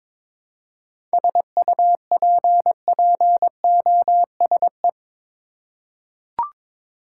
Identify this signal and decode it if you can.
Morse code, heard as a keyed tone, about 22 wpm, 700 Hz. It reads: SUPPOSE
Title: Seven Letter Words - 22wpm